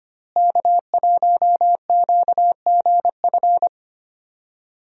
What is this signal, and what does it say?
Morse code, heard as a keyed tone, about 25 wpm, 700 Hz. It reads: K1QGF
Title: USA Call Signs - 5 Characters - Repeated 3x - 25wpm